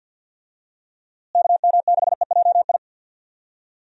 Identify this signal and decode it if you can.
Morse code, heard as a keyed tone, about 52 wpm, 700 Hz. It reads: KM6EJI
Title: USA Call Signs - 6 Characters - Repeated 3x - 50wpm